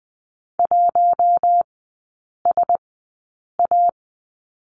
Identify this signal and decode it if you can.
Morse code, heard as a keyed tone, about 20 wpm, spaced as for 8 wpm, 700 Hz. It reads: 1SA
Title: 047 Lesson 9e - Number 4 - Three Characters 20wpm